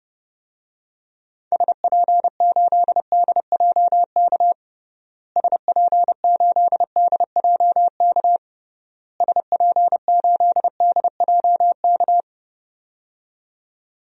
Morse code, heard as a keyed tone, about 32 wpm, 700 Hz. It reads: SP8DJK SP8DJK SP8DJK
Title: DXCC-100 Call Signs - 6 Characters - Repeated 3x - 30wpm